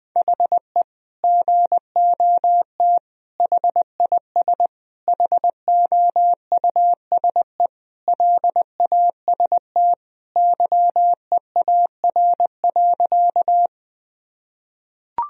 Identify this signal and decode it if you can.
Morse code, heard as a keyed tone, about 20 wpm, 700 Hz. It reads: HE GOT HIS HOUSE LAST YEAR.
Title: Sentences from Top 1000 Words - Review Part 1 - 20wpm